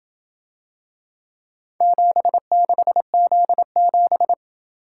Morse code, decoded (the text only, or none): 76Z7